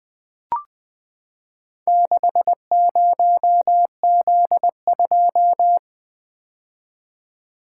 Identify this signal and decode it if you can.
Morse code, heard as a keyed tone, about 20 wpm, 700 Hz. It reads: 60Z2
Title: Four Letter-Number Combinations - 20wpm